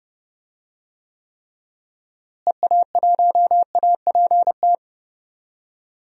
Morse code, heard as a keyed tone, about 32 wpm, 700 Hz. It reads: EA1APT
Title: DXCC-100 Call Signs - 6 Characters - Repeated 3x - 30wpm